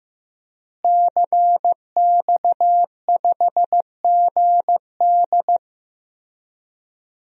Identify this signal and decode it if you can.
Morse code, heard as a keyed tone, about 15 wpm, 700 Hz. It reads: CX5GD
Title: DXCC-100 Call Signs - 5 Characters - Repeated 2x - 15wpm